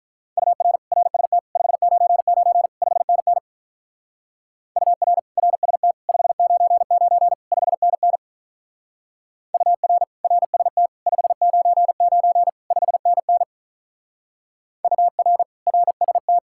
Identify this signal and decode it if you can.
Morse code, heard as a keyed tone, about 43 wpm, 700 Hz. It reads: UR RST 599 5NN UR RST 599 5NN UR RST 599 5NN UR RST
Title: Speed-Racing Common QSO Phrases - 35wpm